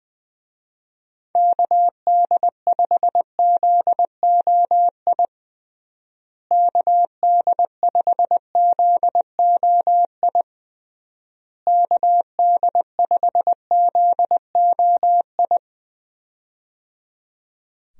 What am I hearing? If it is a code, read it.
Morse code, heard as a keyed tone, about 20 wpm, 700 Hz. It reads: KD5ZOI KD5ZOI KD5ZOI
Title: USA Call Signs - 6 Characters - Repeated 3x - 20wpm